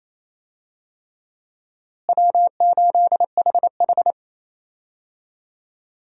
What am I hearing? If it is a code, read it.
Morse code, heard as a keyed tone, about 28 wpm, 700 Hz. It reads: W8HH
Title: USA Call Signs - 4 Characters - Repeated 3x - 28wpm